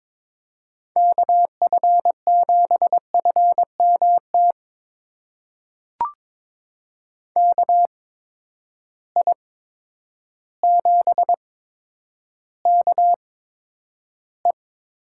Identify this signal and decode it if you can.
Morse code, heard as a keyed tone, about 22 wpm, spaced as for 10 wpm, 700 Hz. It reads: KF7FMT KI7KE
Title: USA Call Signs - 6 Characters - 8x Character Spacing - 22wpm